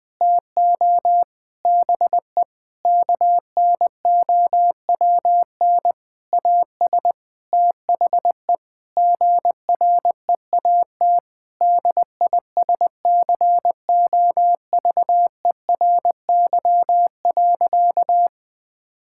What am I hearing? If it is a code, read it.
Morse code, heard as a keyed tone, about 19 wpm, 700 Hz. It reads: TO BE KNOWN AS THE GREAT DISCOVERY.
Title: Sentences with 8-Word Phrases - Spoken Beginning - 20wpm